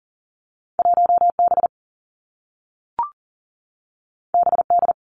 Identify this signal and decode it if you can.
Morse code, heard as a keyed tone, about 40 wpm, 700 Hz. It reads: 1 B B D
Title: Two Challenging Letter-Number Combinations 40wpm